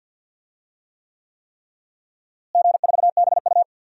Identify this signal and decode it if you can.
Morse code, heard as a keyed tone, about 52 wpm, 700 Hz. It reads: G4BU
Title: DXCC-100 Call Signs - 4 Characters - Repeated 3x - 50wpm